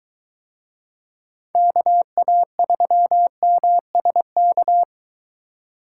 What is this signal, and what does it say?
Morse code, heard as a keyed tone, about 23 wpm, 700 Hz. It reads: KA3MSK